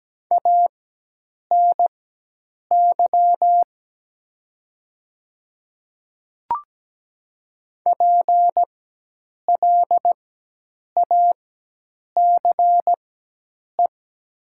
Morse code, heard as a keyed tone, about 17 wpm, spaced as for 8 wpm, 700 Hz. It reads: ANY PLACE
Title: 4x Character Spacing - Top 200 Words - 17wpm